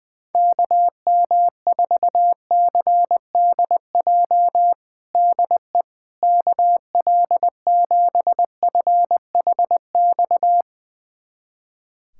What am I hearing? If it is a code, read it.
Morse code, heard as a keyed tone, about 20 wpm, 700 Hz. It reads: KM4CDJ DE KL7FHX
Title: USA Callsign DE Callsign - 6 Characters - 20wpm